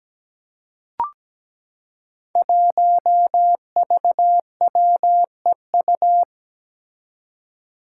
Morse code, heard as a keyed tone, about 17 wpm, 700 Hz. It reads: 1VWEU
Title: Five Challenging Letter-Number Combinations 17wpm